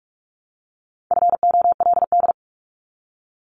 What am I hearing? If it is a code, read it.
Morse code, heard as a keyed tone, about 46 wpm, 700 Hz. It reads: FOLD
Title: Multi-letter Phonemes Repeated 3x 45wpm